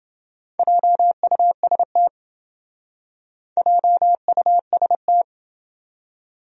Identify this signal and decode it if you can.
Morse code, heard as a keyed tone, about 27 wpm, 700 Hz. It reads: JUST JUST